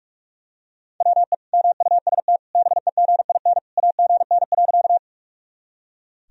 Morse code, heard as a keyed tone, about 42 wpm, 700 Hz. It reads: WE MUST BEGIN AGN.
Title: Sentences from Top 300 Words and Abbreviations 45wpm